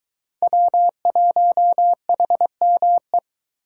Morse code, heard as a keyed tone, about 20 wpm, 700 Hz. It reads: W1HME